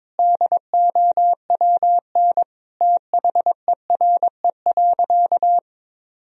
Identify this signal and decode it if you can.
Morse code, heard as a keyed tone, about 22 wpm, 700 Hz. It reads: DOWN THERE.
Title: Sentences from Top 1000 Words - Review Part 4 - 22wpm